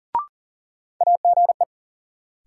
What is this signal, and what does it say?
Morse code, heard as a keyed tone, about 40 wpm, 700 Hz. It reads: AGE
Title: Sentences from Top 500 Words - New Word Drill - 40wpm